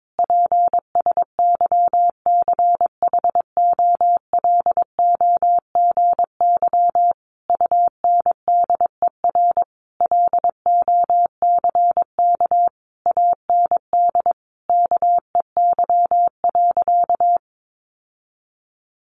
Morse code, heard as a keyed tone, about 22 wpm, 700 Hz. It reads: PSYCHOLOGY UNDER LOCK AND KEY.
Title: Sentences from Top 1400 Words 22wpm